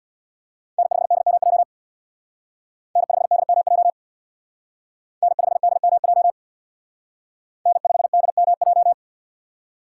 Morse code, heard as a keyed tone, about 62 wpm, 700 Hz. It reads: N5DMJ N5DMJ N5DMJ N5DMJ
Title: Speed Racing - USA 1x3 Call Signs 50wpm